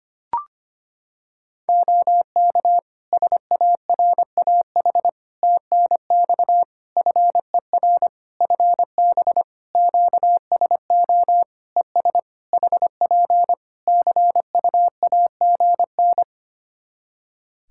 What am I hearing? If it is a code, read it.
Morse code, heard as a keyed tone, about 25 wpm, 700 Hz. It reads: OK SARAH TNX FER FB QSO ES HP CUAGN
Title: Practice QSO - Exchange 4 - Ending - 25wpm